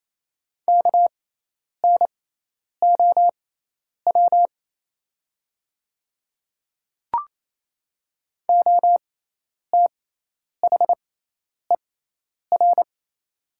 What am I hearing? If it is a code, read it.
Morse code, heard as a keyed tone, about 28 wpm, spaced as for 10 wpm, 700 Hz. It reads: KNOW OTHER
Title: 6x Character Spacing - Top 200 Words - 28wpm